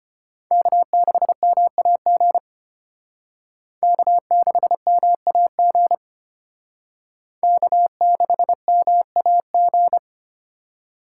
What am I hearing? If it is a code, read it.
Morse code, heard as a keyed tone, about 31 wpm, 700 Hz. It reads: K6MAG K6MAG K6MAG
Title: Speed Racing - USA 1x3 Call Signs 25wpm